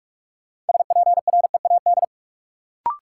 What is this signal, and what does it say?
Morse code, heard as a keyed tone, about 44 wpm, 700 Hz. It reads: SPREAD
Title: Sentences from Top 1200 Words - New Word Drill Alt - 45wpm